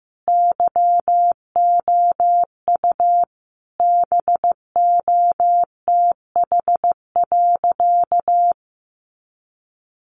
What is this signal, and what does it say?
Morse code, heard as a keyed tone, about 15 wpm, 700 Hz. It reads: YOU BOTH.
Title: Sentences from Top 300 Words 15wpm